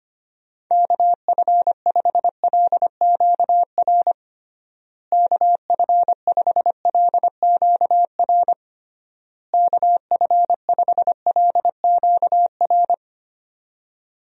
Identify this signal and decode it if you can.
Morse code, heard as a keyed tone, about 25 wpm, 700 Hz. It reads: KF5LQR KF5LQR KF5LQR
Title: USA Call Signs - 6 Characters - Repeated 3x - 25wpm